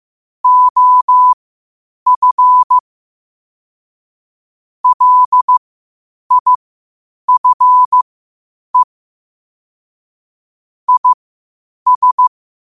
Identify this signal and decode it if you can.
Morse code, heard as a keyed tone, about 15 wpm, spaced as for 8 wpm, 1000 Hz. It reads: OF LIFE IS
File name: qotdfiles / 8wpm